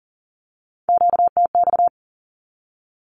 Morse code, heard as a keyed tone, about 40 wpm, 700 Hz. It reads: QTX
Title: Q-Codes 40wpm